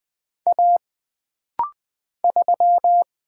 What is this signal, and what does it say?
Morse code, heard as a keyed tone, about 20 wpm, 700 Hz. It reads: A 3
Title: Single Letter-Number Alt - Rapid-Fire 20wpm